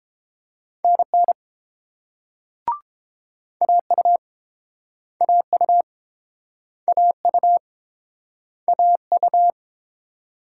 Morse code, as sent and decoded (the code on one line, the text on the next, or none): -. -. / .- ..- / .- ..- / .- ..- / .- ..-
NN AU AU AU AU